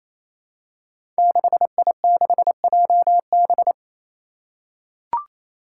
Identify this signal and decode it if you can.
Morse code, heard as a keyed tone, about 27 wpm, 700 Hz. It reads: 6I6JB
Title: Five Challenging Letter-Number Combinations 28wpm